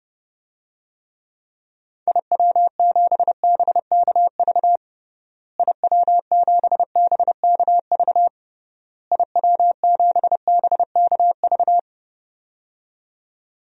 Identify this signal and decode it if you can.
Morse code, heard as a keyed tone, about 30 wpm, 700 Hz. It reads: IW7BKV IW7BKV IW7BKV
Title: DXCC-100 Call Signs - 6 Characters - Repeated 3x - 30wpm